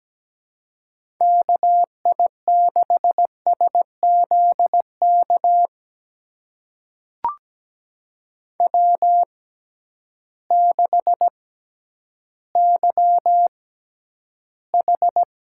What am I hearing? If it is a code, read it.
Morse code, heard as a keyed tone, about 17 wpm, spaced as for 10 wpm, 700 Hz. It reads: KI6SZK W6YH
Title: USA Call Signs - 6x Character Spacing - 17wpm